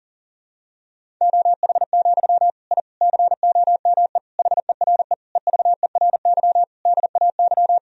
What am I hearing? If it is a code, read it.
Morse code, heard as a keyed tone, about 40 wpm, 700 Hz. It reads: OH, I COME HERE EVERY DAY
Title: Sentences from Top 1000 Words - Review Part 2 - 40wpm